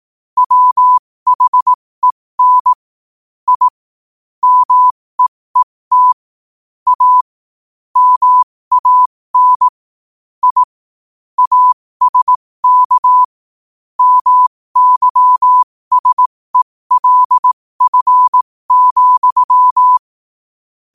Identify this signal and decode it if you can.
Morse code, heard as a keyed tone, about 18 wpm, spaced as for 15 wpm, 1000 Hz. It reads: WHEN I MEET A MAN I ASK MYSELF,
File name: qotdfiles / 15wpm